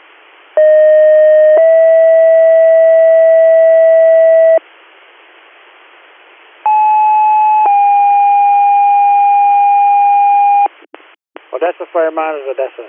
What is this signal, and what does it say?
Two-tone page, heard as a one-second tone, then a three-second tone, about 620 then 650 Hz, about 860 then 810 Hz, with speech. It is heard from 0.6 to 10.7 s.